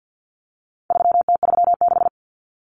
Morse code, heard as a keyed tone, about 50 wpm, 700 Hz. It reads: 3T36
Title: Common Cut Numbers - 4-Digit 50wpm